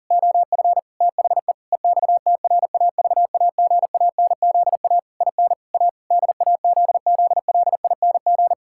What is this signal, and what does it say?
Morse code, heard as a keyed tone, about 40 wpm, 700 Hz. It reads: OF THE EXTRAVAGANZA IN A DAZZLING
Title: Pangram Challenge v2 - 40wpm